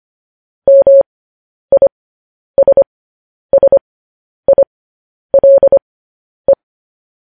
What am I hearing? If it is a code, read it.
Morse code, heard as a keyed tone, about 25 wpm, spaced as for 10 wpm, 550 Hz. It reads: MISSILE